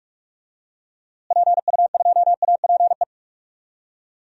Morse code, heard as a keyed tone, about 41 wpm, 700 Hz. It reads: PU2APE
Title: DXCC-100 Call Signs - 6 Characters - Repeated 3x - 45wpm